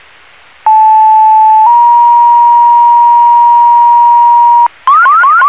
Two-tone page, heard as a one-second tone, then a three-second tone, about 870 then 950 Hz. It is heard from 0.7 to 4.7 s.